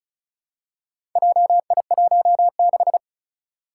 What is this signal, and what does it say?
Morse code, heard as a keyed tone, about 36 wpm, 700 Hz. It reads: JI16
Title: Four Challenging Letter-Number Combinations 35wpm